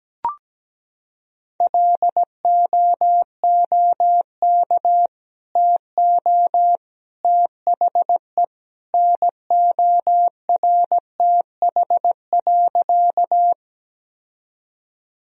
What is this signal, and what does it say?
Morse code, heard as a keyed tone, about 17 wpm, 700 Hz. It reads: LOOK TO THE NORTH.
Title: Sentences from Top 1000 Words - Review Part 1 - 17wpm